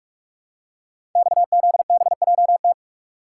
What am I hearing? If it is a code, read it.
Morse code, heard as a keyed tone, about 46 wpm, 700 Hz. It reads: XZBJT
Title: Code Groups - 5 Letter:Numbers - 45wpm